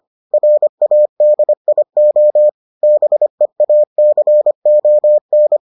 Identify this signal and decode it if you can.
Morse code, heard as a keyed tone, about 25 wpm, 600 Hz. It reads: RADIO BEACON